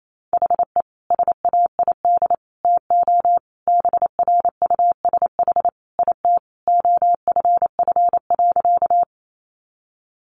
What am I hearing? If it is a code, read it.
Morse code, heard as a keyed tone, about 28 wpm, 700 Hz. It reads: HE SAID TO BRUSH IT OFF.
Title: Sentences from Top 1100 Words 28wpm